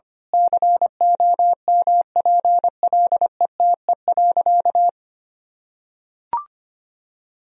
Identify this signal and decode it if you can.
Morse code, heard as a keyed tone, about 25 wpm, 700 Hz. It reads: COMPLETE.
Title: Sentences from Top 1000 Words - Review Part 2 - 25wpm